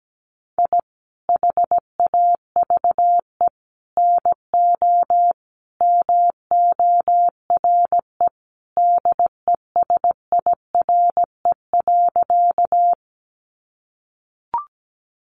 Morse code, heard as a keyed tone, about 17 wpm, 700 Hz. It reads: I HAVE NO MORE DESIRE.
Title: Sentences from Top 1100 Words 17wpm